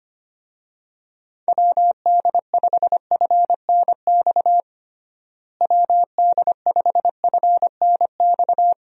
Morse code, heard as a keyed tone, about 25 wpm, 700 Hz. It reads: WD5FNX WD5FNX
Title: USA Call Signs - 6 Characters - Repeated 2x - 25wpm